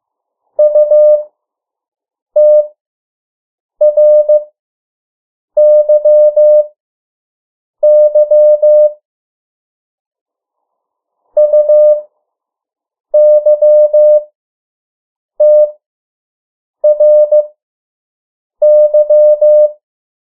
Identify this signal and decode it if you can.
Morse code, heard as a keyed tone, about 15 wpm, spaced as for 6 wpm, 600 Hz. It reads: UTRYY UYTRY
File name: morse mp3 / 03 urok 2